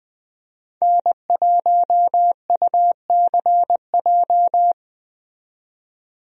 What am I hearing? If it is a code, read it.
Morse code, heard as a keyed tone, about 21 wpm, 700 Hz. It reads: N1UCJ
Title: USA Call Signs - 5 Characters - 20wpm